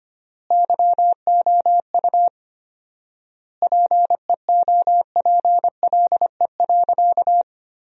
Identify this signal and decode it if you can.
Morse code, heard as a keyed tone, about 25 wpm, 700 Hz. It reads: YOU PEOPLE.
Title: 4x Word-Spacing - Sentences from Top 300 Words - 25wpm